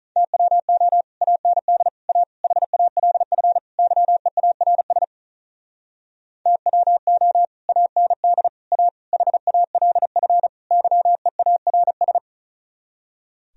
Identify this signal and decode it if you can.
Morse code, heard as a keyed tone, about 43 wpm, 700 Hz. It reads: TWO AND A HALF YEARS TWO AND A HALF YEARS
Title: Speed-Racing - Sets of 5 Words 35wpm